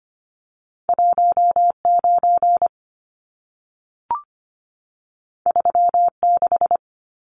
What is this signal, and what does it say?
Morse code, heard as a keyed tone, about 25 wpm, 700 Hz. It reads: 19 36
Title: Numbers - 2-Digit 25wpm